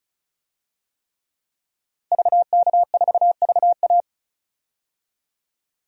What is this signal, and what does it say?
Morse code, heard as a keyed tone, about 34 wpm, 700 Hz. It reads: VK4VA
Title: DXCC-100 Call Signs - 5 Characters - Repeated 3x - 35wpm